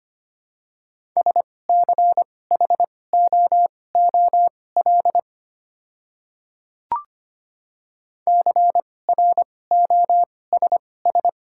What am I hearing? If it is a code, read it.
Morse code, heard as a keyed tone, about 25 wpm, spaced as for 18 wpm, 690 Hz. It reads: SCHOOL CROSS
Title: 2x Character Spacing - Top 300 Words - 25wpm